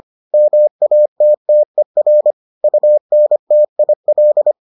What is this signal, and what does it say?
Morse code, heard as a keyed tone, about 25 wpm, 600 Hz. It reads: MATTER UNTIL